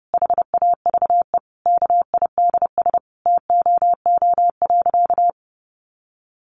Morse code, heard as a keyed tone, about 30 wpm, 700 Hz. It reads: HAVE KIDS TOO.
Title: Easy Phrases 30wpm